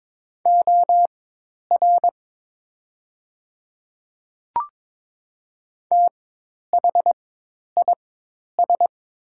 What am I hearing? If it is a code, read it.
Morse code, heard as a keyed tone, about 22 wpm, spaced as for 10 wpm, 690 Hz. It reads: OR THIS
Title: 4x Character Spacing - Top 100 Words - 22wpm